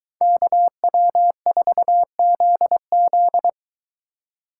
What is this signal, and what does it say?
Morse code, heard as a keyed tone, about 22 wpm, 700 Hz. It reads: KW4ZZ